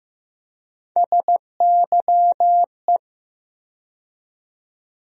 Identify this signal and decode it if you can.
Morse code, heard as a keyed tone, about 15 wpm, 700 Hz. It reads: SYE
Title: USA Call Sign Postfixes - 3 Letters 15wpm